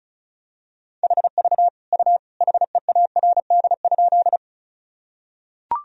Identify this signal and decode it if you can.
Morse code, heard as a keyed tone, about 34 wpm, 700 Hz. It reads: HV U HEARD?
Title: Sentences from Top 400 Words and Abbreviations 35wpm